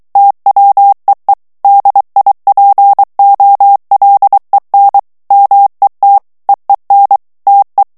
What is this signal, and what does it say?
Morse code, heard as a keyed tone, about 23 wpm, 800 Hz. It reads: TWEE DIPOLEN MET EEN TE
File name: HOLLANDAIS texte clair 1 vitesse 22 mots minute